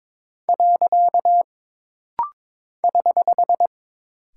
Morse code, heard as a keyed tone, about 22 wpm, 700 Hz. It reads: . <HH>
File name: practice / 22wpm / Single Punctuation-Prosign - Rapid-Fire - 22wpm